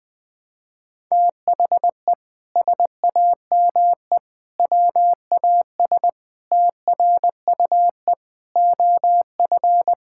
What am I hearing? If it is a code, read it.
Morse code, heard as a keyed tone, about 20 wpm, 700 Hz. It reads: THE SAME WAS TRUE OF
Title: Sentences from Top 1000 Words - Review Part 6 - 20wpm